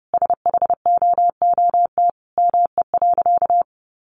Morse code, heard as a keyed tone, about 30 wpm, 700 Hz. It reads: SHOOT ME.
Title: Sentences from Top 1500 Words 30wpm